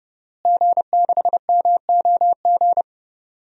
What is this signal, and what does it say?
Morse code, heard as a keyed tone, about 31 wpm, 700 Hz. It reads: G6MOG
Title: DXCC-100 Call Signs - 5 Characters - Repeated 3x - 30wpm